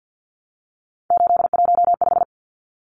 Morse code, heard as a keyed tone, about 50 wpm, 700 Hz. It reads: 815
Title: Numbers - 3-Digit 50wpm